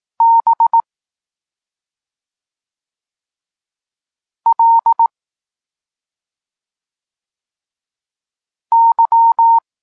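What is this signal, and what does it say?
Morse code, heard as a keyed tone, about 18 wpm, 920 Hz. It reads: B L Y
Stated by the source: Polish Morse lesson teaching the letter C, from SP9MOA Niepołomicki Klub Krótkofalowców.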